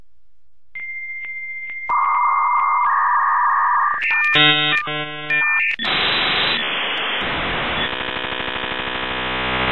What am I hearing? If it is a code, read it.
Dial-up modem handshake.